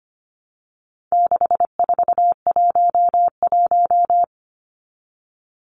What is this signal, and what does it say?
Morse code, heard as a keyed tone, about 25 wpm, 700 Hz. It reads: 6411
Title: Numbers - 4-Digit 25wpm